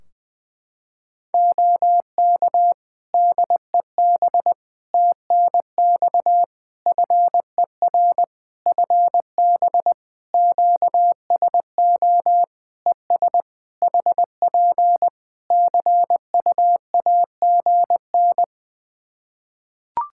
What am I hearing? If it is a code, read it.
Morse code, heard as a keyed tone, about 20 wpm, 700 Hz. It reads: OK DEB TNX FER FB QSO ES HP CUAGN
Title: Practice QSO - Exchange 4 - Ending - 20wpm